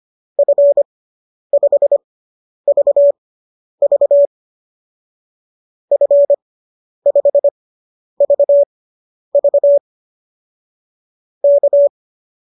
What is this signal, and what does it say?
Morse code, heard as a keyed tone, about 25 wpm, spaced as for 10 wpm, 590 Hz. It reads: F5VV F5VV K